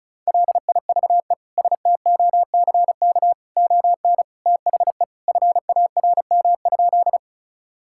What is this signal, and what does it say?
Morse code, heard as a keyed tone, about 36 wpm, 700 Hz. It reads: LIVESTOCK ON THE FARM?
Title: Sentences from Top 1000 Words - Review Part 5 - 35wpm